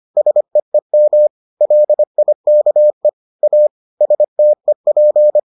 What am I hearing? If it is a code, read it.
Morse code, heard as a keyed tone, about 25 wpm, 600 Hz. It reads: SEEM LIKE A STEP